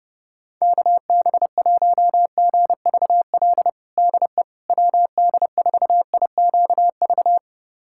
Morse code, heard as a keyed tone, about 32 wpm, 700 Hz. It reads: KB1GVL DE WD4IQV
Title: USA Callsign DE Callsign - 6 Characters - 30wpm